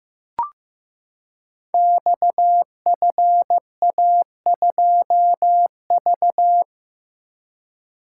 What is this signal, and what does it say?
Morse code, heard as a keyed tone, about 15 wpm, 700 Hz. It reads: XFA2V
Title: Code Groups - 5 Letter:Numbers - 15wpm